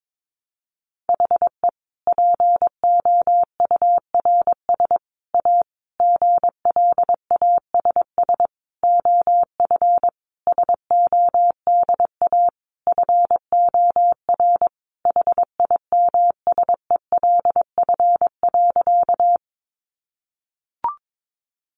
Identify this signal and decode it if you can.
Morse code, heard as a keyed tone, about 22 wpm, 700 Hz. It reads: HE POURS A GLASS OF SODA FOR HIMSELF.
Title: Additional Easy Phrases 22wpm